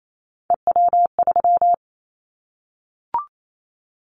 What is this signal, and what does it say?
Morse code, heard as a keyed tone, about 28 wpm, 700 Hz. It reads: EW3
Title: Callsign Prefixes for Top 100 DXCC - 28wpm